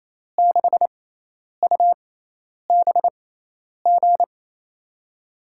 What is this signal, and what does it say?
Morse code, heard as a keyed tone, about 28 wpm, spaced as for 10 wpm, 700 Hz. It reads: 6UBG
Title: USA Call Signs - 6 Characters - 6x Character Spacing - 28wpm